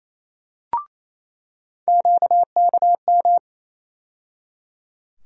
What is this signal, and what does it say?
Morse code, heard as a keyed tone, about 28 wpm, 700 Hz. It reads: QKM
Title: USA Call Sign Postfixes - 3 Letters 28wpm